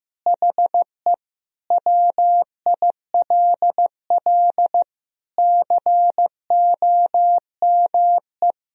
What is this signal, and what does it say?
Morse code, heard as a keyed tone, about 15 wpm, 700 Hz. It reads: HE WILL COME
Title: Sentences from Top 1000 Words - Review Part 3 - 15wpm